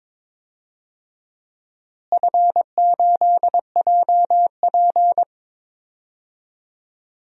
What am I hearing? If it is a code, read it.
Morse code, heard as a keyed tone, about 22 wpm, 700 Hz. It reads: F8JP
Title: DXCC-100 Call Signs - 4 Characters - Repeated 2x - 22wpm